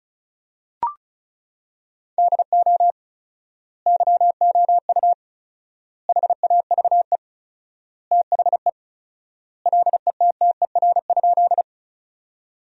Morse code, heard as a keyed tone, about 34 wpm, 700 Hz. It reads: DO YOU HAVE THE LETTER?
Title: 4x Word-Spacing - Sentences from Top 300 Words - 35wpm